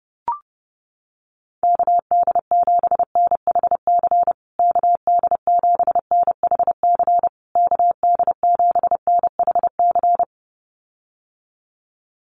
Morse code, heard as a keyed tone, about 30 wpm, 700 Hz. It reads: KD7NHC KD7NHC KD7NHC
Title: USA Call Signs Repeated 3x 30wpm